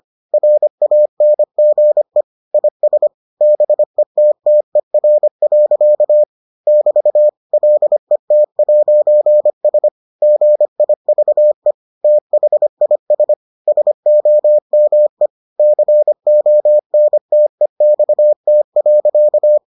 Morse code, heard as a keyed tone, about 25 wpm, 600 Hz. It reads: RANGE IS BETTER. <BT> LET'S GIVE THIS SOME CONTEXT.